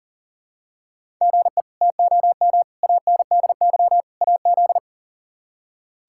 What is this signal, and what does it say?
Morse code, heard as a keyed tone, about 40 wpm, 700 Hz. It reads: GE TOM ANDY AZ
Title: K1USN SST QSO Practice - 40wpm